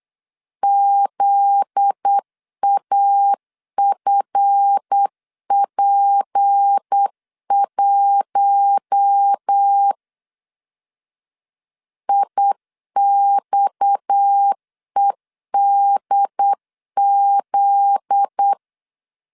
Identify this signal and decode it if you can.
Morse code, heard as a keyed tone, about 8 wpm, 800 Hz. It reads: ZAFP1 IXEDZ
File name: LEZIONI / 7 GF9Z / 7 GF9Z 8wpm 3213